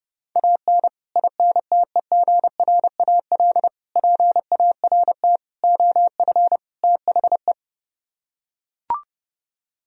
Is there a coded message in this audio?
Yes: Morse code from 0.4 to 7.5 s, heard as a keyed tone, about 30 wpm, 700 Hz. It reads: AN INTEGRAL PART OF THE